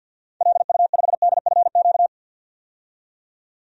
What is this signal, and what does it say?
Morse code, heard as a keyed tone, about 47 wpm, 700 Hz. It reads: LU5DFQ
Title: DXCC-100 Call Signs - 6 Characters - 50wpm